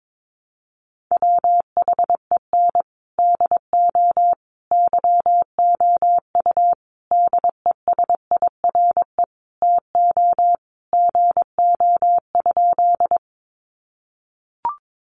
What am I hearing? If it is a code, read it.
Morse code, heard as a keyed tone, about 22 wpm, 700 Hz. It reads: WHEN DO YOU DESIRE TO GO?
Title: Sentences from Top 1100 Words 22wpm